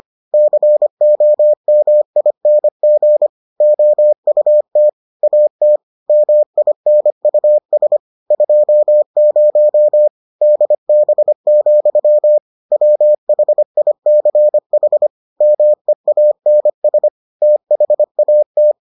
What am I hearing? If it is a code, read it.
Morse code, heard as a keyed tone, about 25 wpm, 600 Hz. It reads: COMING OUT AT MINUS 20 DB, WHICH MEANS THAT